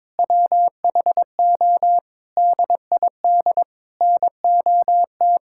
Morse code, heard as a keyed tone, about 22 wpm, 700 Hz. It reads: WHO DID NOT
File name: Sentences with 7-Word Phrases - 22wpm